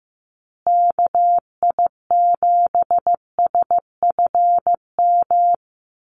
Morse code, heard as a keyed tone, about 15 wpm, 700 Hz. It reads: KI7SFM
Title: USA Call Signs Repeated 3x 15wpm